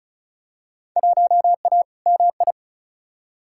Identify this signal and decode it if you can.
Morse code, heard as a keyed tone, about 36 wpm, 700 Hz. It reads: 1A MI
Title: Field Day Exchanges - 35wpm